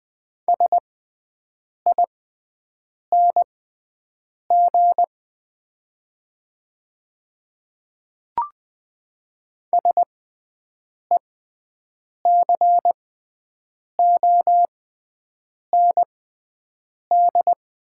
Morse code, heard as a keyed tone, about 20 wpm, spaced as for 7 wpm, 700 Hz. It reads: SING SECOND
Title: 6x Character Spacing - Top 500 Words - 20wpm